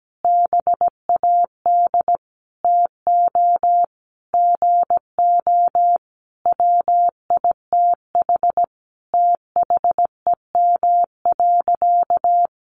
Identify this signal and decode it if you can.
Morse code, heard as a keyed tone, about 17 wpm, 700 Hz. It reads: BAD TO GO WITH THEM.
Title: Sentences from Top 700 Words 17wpm